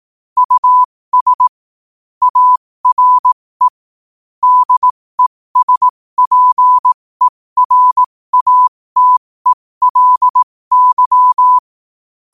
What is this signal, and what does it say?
Morse code, heard as a keyed tone, about 18 wpm, spaced as for 15 wpm, 1000 Hz. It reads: US ARE DESPERATELY
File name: qotdfiles / 15wpm